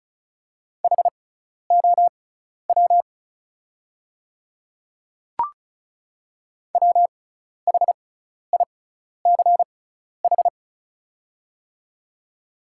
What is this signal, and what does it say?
Morse code, heard as a keyed tone, about 36 wpm, spaced as for 12 wpm, 700 Hz. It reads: HOW WHICH
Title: 6x Character Spacing - Top 100 Words - 35wpm